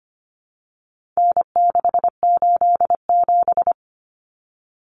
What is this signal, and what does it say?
Morse code, heard as a keyed tone, about 25 wpm, 700 Hz. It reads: N687
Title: Common Cut Numbers - 4-Digit 25wpm